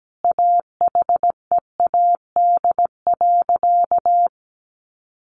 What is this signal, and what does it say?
Morse code, heard as a keyed tone, about 17 wpm, 700 Hz. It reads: AHEAD.
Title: Sentences from Top 1100 Words 17wpm